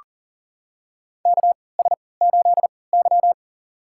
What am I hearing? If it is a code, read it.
Morse code, heard as a keyed tone, about 38 wpm, spaced as for 22 wpm, 700 Hz. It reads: KS8Y
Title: USA Call Signs - 4 Characters - 3x Character Spacing - 40wpm